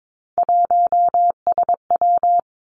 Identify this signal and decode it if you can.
Morse code, heard as a keyed tone, about 22 wpm, 700 Hz. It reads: 1SW
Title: Three Challenging Letter-Number Combinations 22wpm